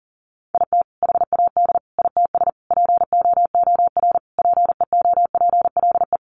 Morse code, heard as a keyed tone, about 40 wpm, 700 Hz. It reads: IT HAD ITS POOR PEOPLE
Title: Sentences from Top 1000 Words 40wpm